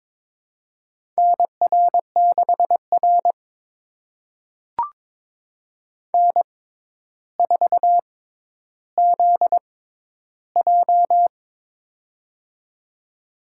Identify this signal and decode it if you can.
Morse code, heard as a keyed tone, about 22 wpm, spaced as for 11 wpm, 700 Hz. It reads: NR6R N4ZJ
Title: USA Call Signs - 4 Characters - 6x Character Spacing - 22wpm